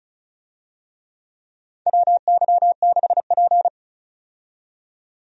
Morse code, heard as a keyed tone, about 36 wpm, 700 Hz. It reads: WY6P
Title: USA Call Signs - 4 Characters - 35wpm